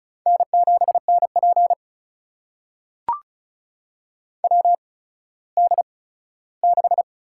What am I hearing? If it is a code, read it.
Morse code, heard as a keyed tone, about 36 wpm, spaced as for 17 wpm, 700 Hz. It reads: N7NP WD6
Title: USA Call Signs - 4 Characters - 8x Character Spacing - 35wpm